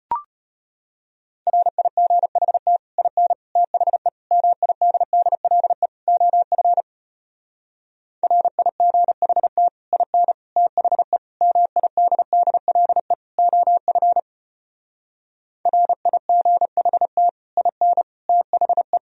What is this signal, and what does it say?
Morse code, heard as a keyed tone, about 34 wpm, 700 Hz. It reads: RIGHT IN THE MIDDLE OF RIGHT IN THE MIDDLE OF RIGHT IN THE